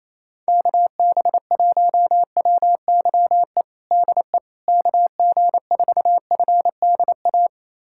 Morse code, heard as a keyed tone, about 28 wpm, 700 Hz. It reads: KB1WYE DE KG4FDA
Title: USA Callsign DE Callsign - 6 Characters - 28wpm